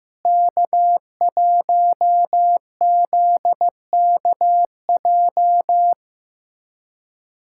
Morse code, heard as a keyed tone, about 15 wpm, 700 Hz. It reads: K1ZKJ
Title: USA Callsign DE Callsign 15wpm